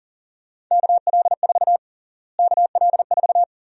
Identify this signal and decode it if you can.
Morse code, heard as a keyed tone, about 37 wpm, 700 Hz. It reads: KL4 KL4
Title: USA Call Sign Prefixes - Repeated 2x - 40wpm